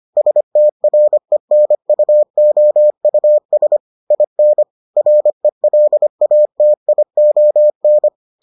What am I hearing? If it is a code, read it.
Morse code, heard as a keyed tone, about 25 wpm, 600 Hz. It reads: STRENUOUS IN RELATION